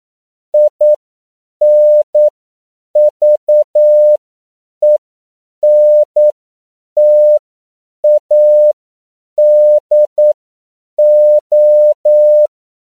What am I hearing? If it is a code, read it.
Morse code, heard as a keyed tone, about 9 wpm, spaced as for 7 wpm, 600 Hz. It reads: INVENTADO
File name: CW / 10PPM